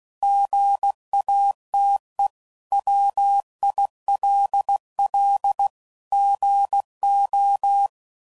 Morse code, heard as a keyed tone, about 16 wpm, 800 Hz. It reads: GATE WILL GO
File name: CW Code Groups @15WPM / Technical Subjects / OR Gate